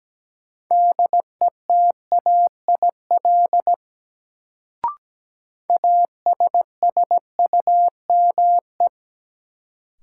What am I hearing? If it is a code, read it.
Morse code, heard as a keyed tone, about 17 wpm, 700 Hz. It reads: DETAIL ASSUME
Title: Sentences from Top 1500 Words - New Word Drill Alt - 17wpm